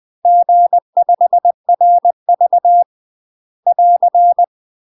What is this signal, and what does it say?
Morse code, heard as a keyed tone, about 20 wpm, 700 Hz. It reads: G5RV <AR>
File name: UK MASTER QSOs TRAINING/20WPM